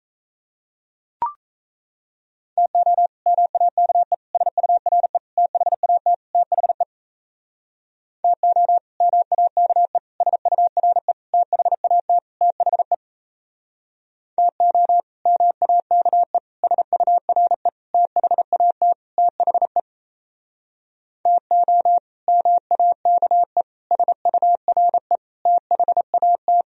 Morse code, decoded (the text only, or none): TO MAKE SURE THAT THE TO MAKE SURE THAT THE TO MAKE SURE THAT THE TO MAKE SURE THAT